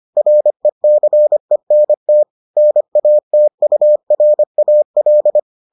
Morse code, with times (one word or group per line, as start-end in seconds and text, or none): 0.2-2.2 RECENT
2.6-5.4 NATURAL